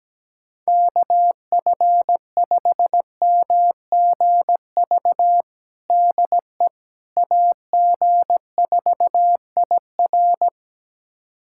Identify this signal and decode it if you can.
Morse code, heard as a keyed tone, about 17 wpm, 700 Hz. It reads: KF5MGV DE AG4IR
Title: USA Callsign DE Callsign 17wpm